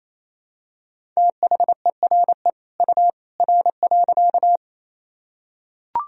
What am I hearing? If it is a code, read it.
Morse code, heard as a keyed tone, about 28 wpm, 700 Hz. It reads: THERE U R.
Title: Sentences from Top 100 Words and Abbreviations 28wpm